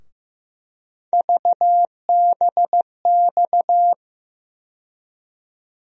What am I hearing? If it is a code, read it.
Morse code, heard as a keyed tone, about 16 wpm, 700 Hz. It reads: VBX
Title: USA Call Sign Postfixes - 3 Letters 15wpm